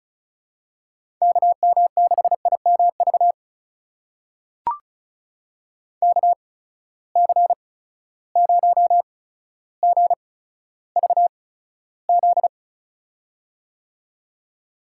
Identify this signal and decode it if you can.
Morse code, heard as a keyed tone, about 36 wpm, spaced as for 15 wpm, 700 Hz. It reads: KM6IMV KC0GVZ
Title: USA Call Signs - 6 Characters - 8x Character Spacing - 35wpm